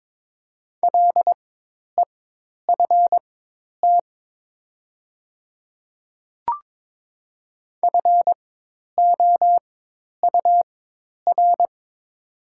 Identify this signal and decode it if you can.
Morse code, heard as a keyed tone, about 22 wpm, spaced as for 10 wpm, 690 Hz. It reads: LEFT FOUR
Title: 4x Character Spacing - Top 300 Words - 22wpm